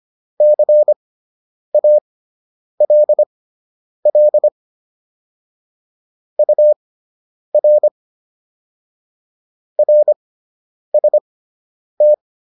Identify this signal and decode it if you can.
Morse code, heard as a keyed tone, about 25 wpm, spaced as for 9 wpm, 600 Hz. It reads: CALL UR RST